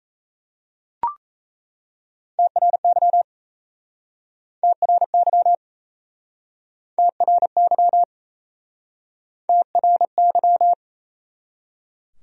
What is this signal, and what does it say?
Morse code, heard as a keyed tone, about 38 wpm, 700 Hz. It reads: TRY TRY TRY TRY